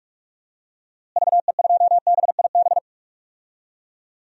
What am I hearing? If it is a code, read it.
Morse code, heard as a keyed tone, about 46 wpm, 700 Hz. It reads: VE2BIB
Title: DXCC-100 Call Signs - 6 Characters - Repeated 2x - 45wpm